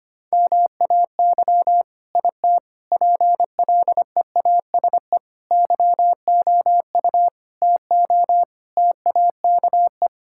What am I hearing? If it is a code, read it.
Morse code, heard as a keyed tone, about 25 wpm, 700 Hz. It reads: MAY IT PLEASE YOU TO TAKE
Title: Sentences from Top 1600 Words 25wpm